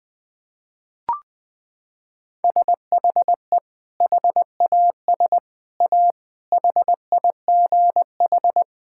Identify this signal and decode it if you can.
Morse code, heard as a keyed tone, about 20 wpm, 700 Hz. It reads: SHE HAS A HIGH
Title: Sentences from Top 1000 Words - Review Part 6 - 20wpm